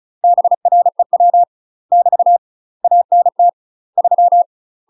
Morse code, heard as a keyed tone, about 36 wpm, 700 Hz. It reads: BREW <BT> ANT 3
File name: UK MASTER QSOs TRAINING / 35WPM / UK MASTER QSOs 0032